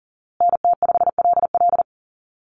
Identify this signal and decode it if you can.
Morse code, heard as a keyed tone, about 40 wpm, 700 Hz. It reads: NT5LL
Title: USA Call Signs Repeated 3x 40wpm